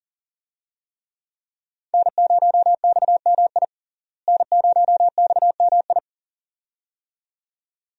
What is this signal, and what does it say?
Morse code, heard as a keyed tone, about 40 wpm, 700 Hz. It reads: N0XMI N0XMI
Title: USA Call Signs - 5 Characters - Repeated 2x - 40wpm